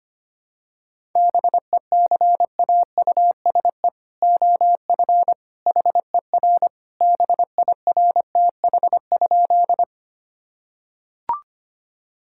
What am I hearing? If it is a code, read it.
Morse code, heard as a keyed tone, about 25 wpm, 700 Hz. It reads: BECAUSE OF HER BIRTH?
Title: Sentences from Top 1000 Words - Review Part 5 - 25wpm